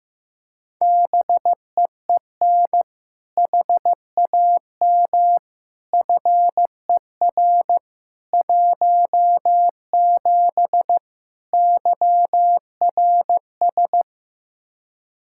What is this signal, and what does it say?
Morse code, heard as a keyed tone, about 16 wpm, 700 Hz. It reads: BEEN HAM FER 17 YRS
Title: K1USN SST QSO Practice 15wpm